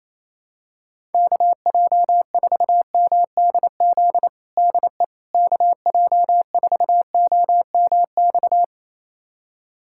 Morse code, heard as a keyed tone, about 28 wpm, 700 Hz. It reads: KJ4MDZ DE KJ4OMX
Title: USA Callsign DE Callsign - 6 Characters - 28wpm